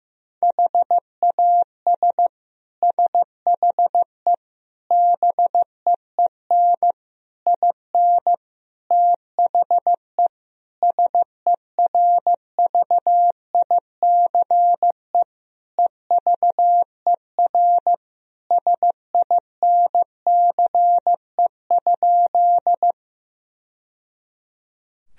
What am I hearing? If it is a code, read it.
Morse code, heard as a keyed tone, about 15 wpm, 700 Hz. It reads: HAS SHE BEEN IN THE SERVICE EVER SINCE?
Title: Sentences from Top 1000 Words - Review Part 4 - 15wpm